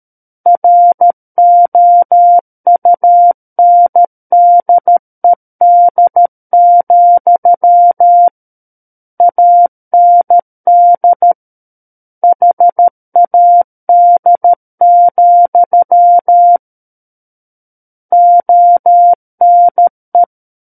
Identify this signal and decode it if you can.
Morse code, heard as a keyed tone, about 13 wpm, 700 Hz. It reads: ROUNDED, AND HAD, ONE
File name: Book One / 04-Chapter 4 - 13 WPM